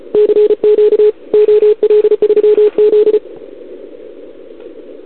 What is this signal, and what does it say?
Morse code, heard as a keyed tone, about 34 wpm, 440 Hz. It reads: CQ OL3Z
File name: CQWWCW2021 / OL3Z